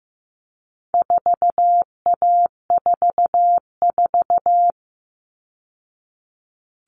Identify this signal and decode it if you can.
Morse code, heard as a keyed tone, about 15 wpm, 700 Hz. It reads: 4A44
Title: All Cut Numbers - 4-Digit 15wpm